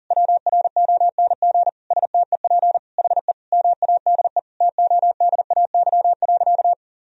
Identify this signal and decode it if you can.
Morse code, heard as a keyed tone, about 40 wpm, 700 Hz. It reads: WRONG STEP HE MADE TODAY.
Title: Sentences from Top 1000 Words - Review Part 2 - 40wpm